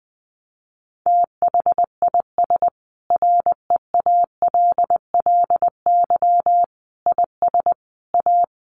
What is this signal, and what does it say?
Morse code, heard as a keyed tone, about 20 wpm, 700 Hz. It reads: THIS REALLY IS A
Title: Sentences from Top 700 Words 20wpm